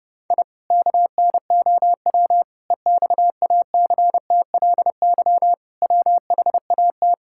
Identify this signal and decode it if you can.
Morse code, heard as a keyed tone, about 30 wpm, 700 Hz. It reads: I KNOW EXACTLY WHAT
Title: Sentences from Top 1000 Words - Review Part 6 - 30wpm